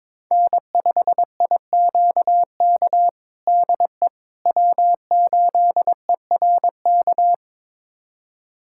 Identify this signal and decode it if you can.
Morse code, heard as a keyed tone, about 22 wpm, 700 Hz. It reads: N5IQK DE W8ERK
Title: USA Callsign DE Callsign - 5 Characters - 22wpm